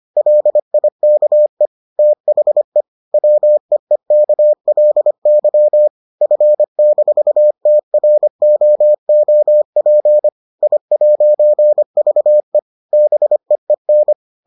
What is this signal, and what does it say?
Morse code, heard as a keyed tone, about 25 wpm, 600 Hz. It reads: LIKE THE WEEKLY F-TROOP I'VE BEEN